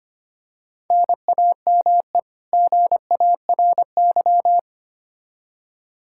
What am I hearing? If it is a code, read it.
Morse code, heard as a keyed tone, about 25 wpm, 700 Hz. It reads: NAME GARY
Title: Practice QSO - Exchange 1 - Beginning - 25wpm